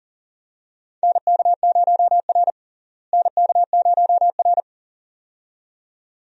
Morse code, heard as a keyed tone, about 38 wpm, 700 Hz. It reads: NK0R NK0R
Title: USA Call Signs - 4 Characters - Repeated 2x - 40wpm